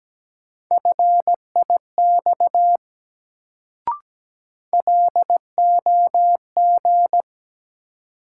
Morse code, heard as a keyed tone, about 17 wpm, 700 Hz. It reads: FIX LOG
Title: Sentences from Top 1400 Words - New Word Drill Alt - 17wpm